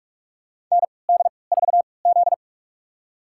Morse code, heard as a keyed tone, about 47 wpm, spaced as for 26 wpm, 700 Hz. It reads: ND4Z
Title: USA Call Signs - 4 Characters - 3x Character Spacing - 45wpm